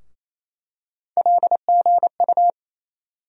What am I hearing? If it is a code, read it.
Morse code, heard as a keyed tone, about 28 wpm, 700 Hz. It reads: LGU
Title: USA Call Sign Postfixes - 3 Letters 28wpm